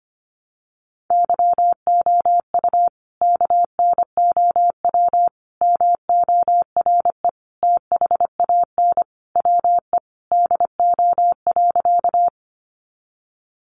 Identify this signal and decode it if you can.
Morse code, heard as a keyed tone, about 25 wpm, 700 Hz. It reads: YOU KNOW MORE THAN WE DO.
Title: Sentences from Top 100 Words 25wpm